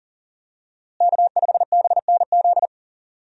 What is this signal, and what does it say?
Morse code, heard as a keyed tone, about 40 wpm, 700 Hz. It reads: K5BNZ
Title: USA Call Signs - 5 Characters - Repeated 2x - 40wpm